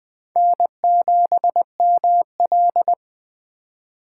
Morse code, heard as a keyed tone, about 20 wpm, 700 Hz. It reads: N7ML
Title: K1USN SST QSO Practice - 20wpm